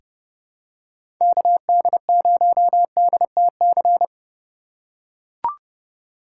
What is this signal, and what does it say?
Morse code, heard as a keyed tone, about 30 wpm, 700 Hz. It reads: KD0DTC
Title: USA Call Signs - 6 Characters - 3x Character Spacing - 30wpm